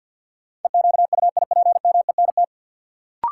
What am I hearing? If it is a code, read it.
Morse code, heard as a keyed tone, about 50 wpm, 700 Hz. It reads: EQUIPMENT
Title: Sentences from Top 700 Words - New Word Drill - 50wpm